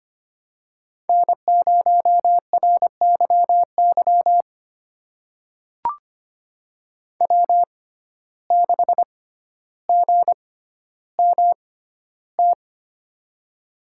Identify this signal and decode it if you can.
Morse code, heard as a keyed tone, about 25 wpm, spaced as for 13 wpm, 700 Hz. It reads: N0RYY W6GMT
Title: USA Call Signs - 5 Characters - 6x Character Spacing - 25wpm